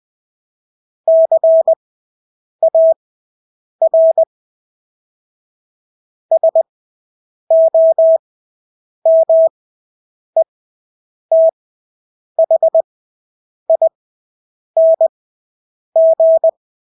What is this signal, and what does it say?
Morse code, heard as a keyed tone, about 20 wpm, spaced as for 8 wpm, 650 Hz. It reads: CAR SOMETHING
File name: Engelske ord 20/8 wpm